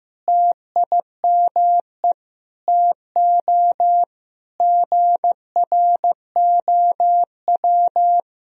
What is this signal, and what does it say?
Morse code, heard as a keyed tone, about 15 wpm, 700 Hz. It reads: TIME TO GROW
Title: Sentences from Top 1000 Words - Review Part 1 - 15wpm